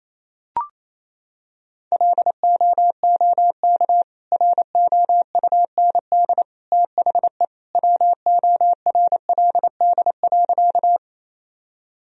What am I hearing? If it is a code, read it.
Morse code, heard as a keyed tone, about 28 wpm, 700 Hz. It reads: LOOK ROUND THE WORLD.
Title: Sentences from Top 1000 Words - Review Part 1 - 28wpm